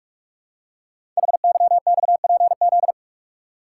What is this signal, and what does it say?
Morse code, heard as a keyed tone, about 44 wpm, 700 Hz. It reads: HYXPZ
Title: Code Groups - 5 Letters - 45wpm